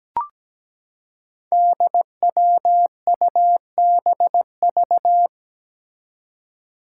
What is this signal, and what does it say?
Morse code, heard as a keyed tone, about 17 wpm, 700 Hz. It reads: DWUBV
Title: Five Challenging Letter-Number Combinations 17wpm